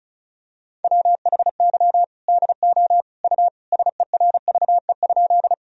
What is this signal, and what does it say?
Morse code, heard as a keyed tone, about 36 wpm, 700 Hz. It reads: WHY DO U SERVE?
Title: Sentences from Top 400 Words and Abbreviations 35wpm